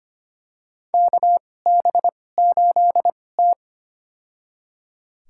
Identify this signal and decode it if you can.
Morse code, heard as a keyed tone, about 25 wpm, spaced as for 18 wpm, 700 Hz. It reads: KB8T
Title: USA Call Signs - 2x Character Spacing - 25wpm